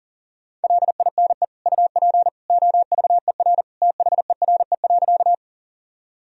Morse code, heard as a keyed tone, about 40 wpm, 700 Hz. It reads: LINE UP OVER THERE.
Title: Sentences from Top 1000 Words - Review Part 1 - 40wpm